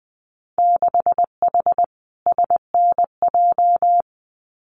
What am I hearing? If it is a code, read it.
Morse code, heard as a keyed tone, about 20 wpm, 700 Hz. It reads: 6H SNJ
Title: Winter Field Day Exchanges Encore 20wpm